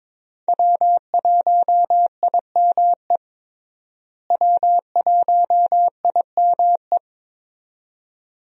Morse code, heard as a keyed tone, about 21 wpm, 700 Hz. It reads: W1IME W1IME
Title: USA Call Signs - 5 Characters - Repeated 2x - 22wpm